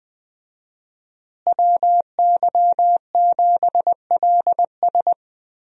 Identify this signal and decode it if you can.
Morse code, heard as a keyed tone, about 20 wpm, 700 Hz. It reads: WY7LS
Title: USA Call Signs - 5 Characters - Repeated 2x - 20wpm